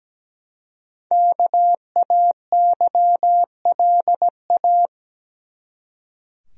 Morse code, heard as a keyed tone, about 17 wpm, 700 Hz. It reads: KAYLA
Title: USA Common Names - 17wpm